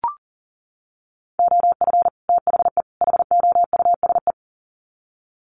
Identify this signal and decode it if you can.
Morse code, heard as a keyed tone, about 40 wpm, 700 Hz. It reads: OF THE HOUSE
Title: Sets of 3 Words 40wpm